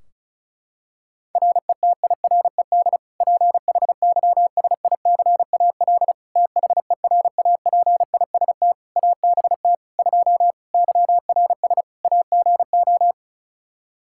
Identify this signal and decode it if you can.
Morse code, heard as a keyed tone, about 34 wpm, 700 Hz. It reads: RETIRED PHYSICAL THERAPIST ABT 2 YRS AGO
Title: K1USN SST QSO Practice 35wpm